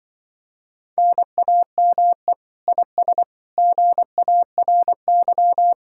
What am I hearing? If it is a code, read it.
Morse code, heard as a keyed tone, about 24 wpm, 700 Hz. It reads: NAME IS GARY
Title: Speed-Racing Common QSO Phrases - 20wpm